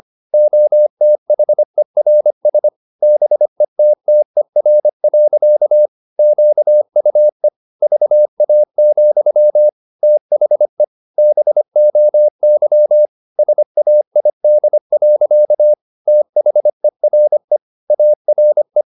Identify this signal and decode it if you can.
Morse code, heard as a keyed tone, about 25 wpm, 600 Hz. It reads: OTHERS BETTER. QUE VA, THE BOY SAID. THERE ARE
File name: the old man and the sea part 1 25wpm